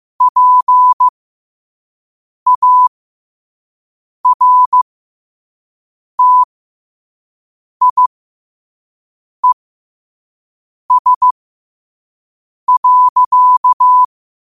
Morse code, heard as a keyed tone, about 15 wpm, spaced as for 5 wpm, 1000 Hz. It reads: PARTIES.
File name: qotdfiles / 5wpm